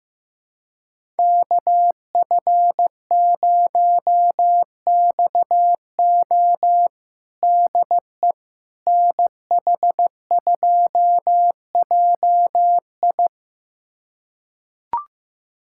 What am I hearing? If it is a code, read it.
Morse code, heard as a keyed tone, about 15 wpm, 700 Hz. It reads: KF0XO DE NH2JI
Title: USA Callsign DE Callsign - 5 Characters - 15wpm